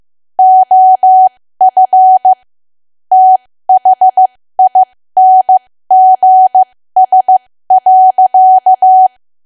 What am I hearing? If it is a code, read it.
Morse code, heard as a keyed tone, about 15 wpm, spaced as for 13 wpm, 750 Hz. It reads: OF THINGS.
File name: cwa / W1AW-Level2 / 13WPM